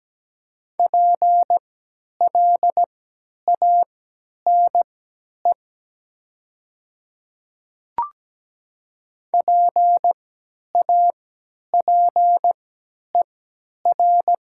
Morse code, heard as a keyed tone, about 17 wpm, spaced as for 10 wpm, 700 Hz. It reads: PLANE PAPER
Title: 3x Character Spacing - Top 500 Words - 17wpm